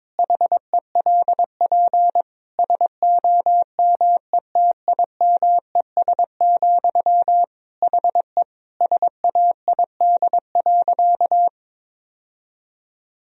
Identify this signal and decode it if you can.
Morse code, heard as a keyed tone, about 21 wpm, 700 Hz. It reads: HELP SOMETIMES, HE SAID.
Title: Sentences from Top 1000 Words - Review Part 3 - 22wpm